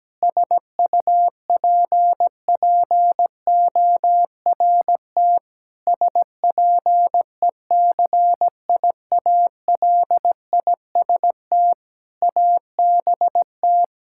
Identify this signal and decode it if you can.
Morse code, heard as a keyed tone, about 17 wpm, 700 Hz. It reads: SUPPORT SPECIALIST ABT
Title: Practice QSO - Exchange 3 - More Info - 17wpm